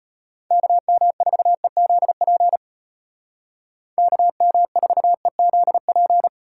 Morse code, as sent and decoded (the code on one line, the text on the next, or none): -.- -- ....- . --.. .--. / -.- -- ....- . --.. .--.
KM4EZP KM4EZP